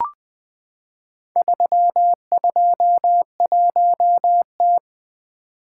Morse code, heard as a keyed tone, about 20 wpm, 700 Hz. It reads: 321T
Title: Four Letter-Number Combinations - 20wpm